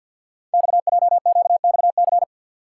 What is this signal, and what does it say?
Morse code, heard as a keyed tone, about 51 wpm, 700 Hz. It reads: XJQXC
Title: Code Groups - 5 Letters - 50wpm